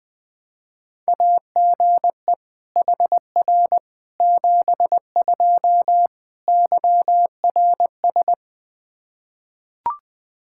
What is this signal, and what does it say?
Morse code, heard as a keyed tone, about 20 wpm, 700 Hz. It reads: AGE HR 72 YRS
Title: Practice QSO - Exchange 3 - More Info - 20wpm